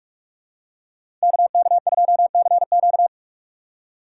Morse code, heard as a keyed tone, about 41 wpm, 700 Hz. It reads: KK2CQ
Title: USA Call Signs - 5 Characters - 3x Character Spacing - 45wpm